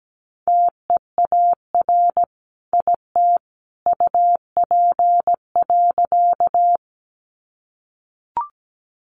Morse code, heard as a keyed tone, about 17 wpm, 700 Hz. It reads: TEAR IT UP.
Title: A Sound Of Thunder 17wpm